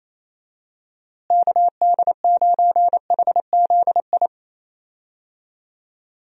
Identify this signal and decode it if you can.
Morse code, heard as a keyed tone, about 28 wpm, 700 Hz. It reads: KD9HZI
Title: USA Call Signs - 6 Characters - Repeated 2x - 28wpm